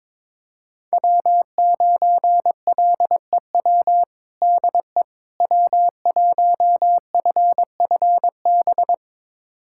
Morse code, heard as a keyed tone, about 22 wpm, 700 Hz. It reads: W9LEW DE W1FFB
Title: USA Callsign DE Callsign - 5 Characters - 22wpm